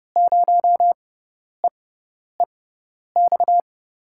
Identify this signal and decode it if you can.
Morse code, heard as a keyed tone, about 32 wpm, spaced as for 10 wpm, 700 Hz. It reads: 0EEX
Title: USA Call Signs - 6 Characters - 6x Character Spacing - 30wpm